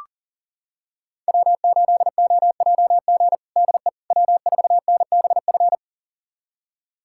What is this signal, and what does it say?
Morse code, heard as a keyed tone, about 40 wpm, 700 Hz. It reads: W8OJG DE W4NBF
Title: USA Callsign DE Callsign - 5 Characters - 40wpm